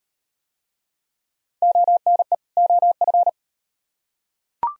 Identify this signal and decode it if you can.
Morse code, heard as a keyed tone, about 38 wpm, 700 Hz. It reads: ONE OF